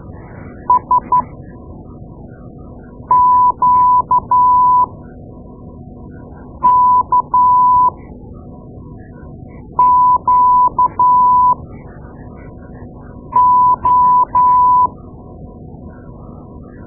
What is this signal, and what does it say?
Morse code, heard as a keyed tone, about 12 wpm, spaced as for 4 wpm, 1000 Hz. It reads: SQKQO
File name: Morse Lessons / Morse Lessons / Tape-2A